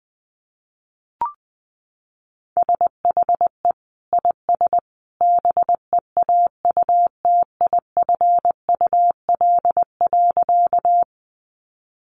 Morse code, heard as a keyed tone, about 20 wpm, 700 Hz. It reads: SHE IS BEAUTIFUL.
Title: Sentences from Top 900 Words 20wpm